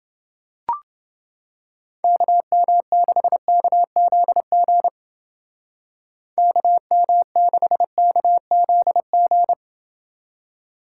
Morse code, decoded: KM6KZG KM6KZG

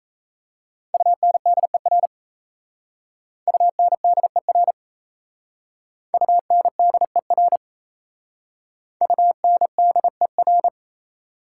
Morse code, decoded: UNDER UNDER UNDER UNDER

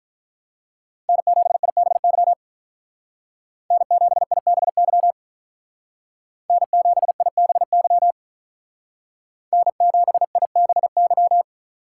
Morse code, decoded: N7IBY N7IBY N7IBY N7IBY

